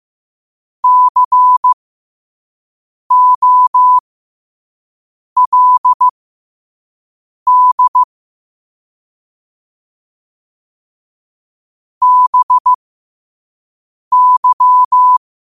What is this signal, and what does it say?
Morse code, heard as a keyed tone, about 15 wpm, spaced as for 5 wpm, 1000 Hz. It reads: COLD BY
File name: qotdfiles / 5wpm